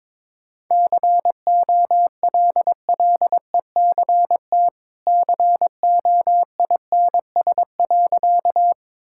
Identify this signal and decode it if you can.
Morse code, heard as a keyed tone, about 22 wpm, 700 Hz. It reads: COLLECT COINS.
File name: practice / 22wpm / Easy Sentences with Hints - 22wpm